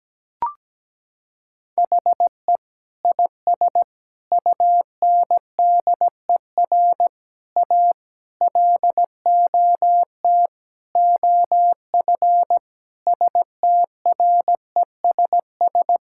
Morse code, heard as a keyed tone, about 17 wpm, 700 Hz. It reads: HE IS UNDER A LOT OF STRESS
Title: Sentences from Top 1000 Words - Review Part 6 - 17wpm